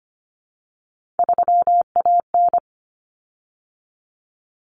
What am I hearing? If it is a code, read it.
Morse code, heard as a keyed tone, about 25 wpm, 700 Hz. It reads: 3AN
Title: Common Cut Numbers - 3-Digit 25wpm